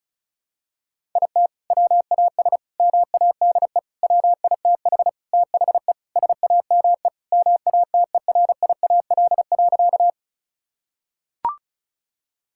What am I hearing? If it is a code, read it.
Morse code, heard as a keyed tone, about 36 wpm, 700 Hz. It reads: IT WAS MADE WITH THE SAME MATERIAL.
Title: Sentences from Top 1000 Words - Review Part 5 - 35wpm